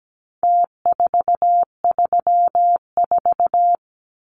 Morse code, heard as a keyed tone, about 17 wpm, 700 Hz. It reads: T434
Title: Numbers - Contest Style - 4-Digit 17wpm